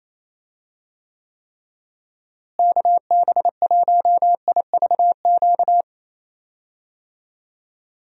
Morse code, heard as a keyed tone, about 28 wpm, 700 Hz. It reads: KB1IVQ
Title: USA Call Signs - 6 Characters - Repeated 3x - 28wpm